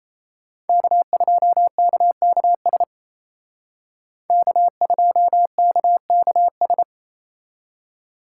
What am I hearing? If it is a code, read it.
Morse code, heard as a keyed tone, about 35 wpm, 700 Hz. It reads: K2KKS K2KKS